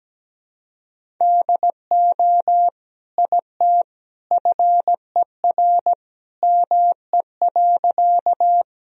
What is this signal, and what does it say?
Morse code, heard as a keyed tone, about 17 wpm, 700 Hz. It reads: DO IT FER ME.
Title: Sentences from Top 200 Words and Abbreviations 17wpm